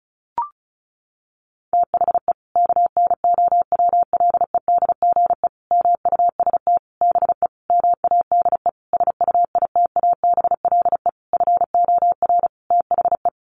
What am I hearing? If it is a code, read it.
Morse code, heard as a keyed tone, about 34 wpm, 700 Hz. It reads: THE KNOWLEDGE MUST BE MADE SUITABLE FOR THE
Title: Sentences from Top 1400 Words 35wpm